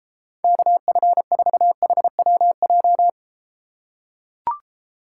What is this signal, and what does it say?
Morse code, heard as a keyed tone, about 34 wpm, 700 Hz. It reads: KF4HWJ